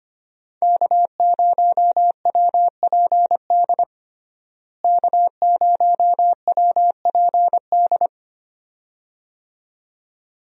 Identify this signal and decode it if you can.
Morse code, heard as a keyed tone, about 25 wpm, 700 Hz. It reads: K0WPD K0WPD
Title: USA Call Signs - 5 Characters - Repeated 2x - 25wpm